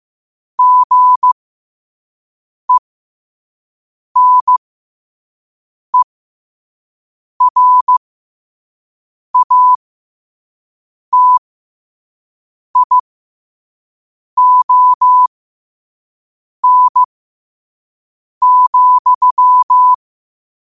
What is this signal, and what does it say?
Morse code, heard as a keyed tone, about 15 wpm, spaced as for 5 wpm, 1000 Hz. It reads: GENERATION,
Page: 5 WPM morse code quotes for Wed, 30 Jul 2025.